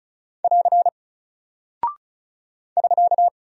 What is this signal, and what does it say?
Morse code, heard as a keyed tone, about 36 wpm, 700 Hz. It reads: <AR> <SK>
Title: Single Character (Letter:Number:Punctuation:Prosign) Alt 35wpm